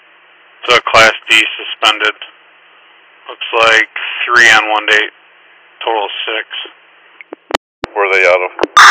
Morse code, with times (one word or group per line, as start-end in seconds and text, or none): none